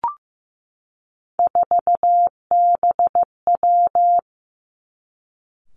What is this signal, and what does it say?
Morse code, heard as a keyed tone, about 15 wpm, 700 Hz. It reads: 4BW
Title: Three Challenging Letter-Number Combinations 15wpm